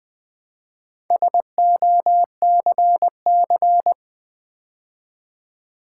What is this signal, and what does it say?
Morse code, heard as a keyed tone, about 20 wpm, 700 Hz. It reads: SOCC
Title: Build A Word - 2 Letters At A Time - 6-Letter Words - 20wpm